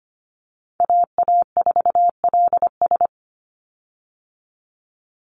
Morse code, heard as a keyed tone, about 25 wpm, 700 Hz. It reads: AA4LS
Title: Call Signs Repeated 2x - 25wpm